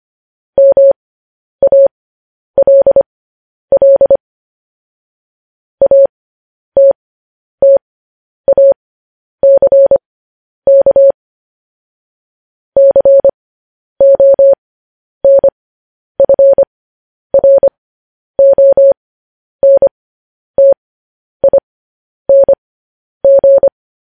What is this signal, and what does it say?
Morse code, heard as a keyed tone, about 25 wpm, spaced as for 10 wpm, 550 Hz. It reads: MALL ATTACK CONFRONTING